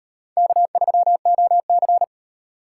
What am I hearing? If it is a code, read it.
Morse code, heard as a keyed tone, about 38 wpm, 700 Hz. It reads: K3OC